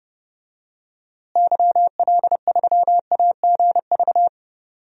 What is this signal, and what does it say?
Morse code, heard as a keyed tone, about 30 wpm, 700 Hz. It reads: YL3AGV
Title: DXCC-100 Call Signs - 6 Characters - Repeated 2x - 30wpm